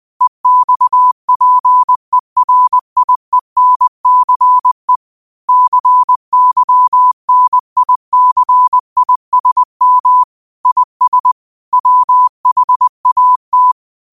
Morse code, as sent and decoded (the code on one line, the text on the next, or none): . -..- .--. .-. .. -. -.-. / -.-. -.-- -. .. -.-. .. ... -- / .. ... / .-- .... .- -
EXPERIENCE CYNICISM IS WHAT